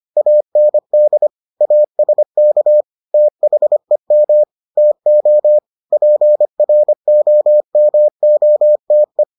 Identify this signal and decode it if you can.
Morse code, heard as a keyed tone, about 25 wpm, 600 Hz. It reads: AND ASK THEM TO PROMOTE